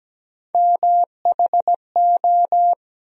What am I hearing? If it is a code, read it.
Morse code, heard as a keyed tone, about 17 wpm, 700 Hz. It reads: MHO
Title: USA Call Sign Postfixes - 3 Letters 17wpm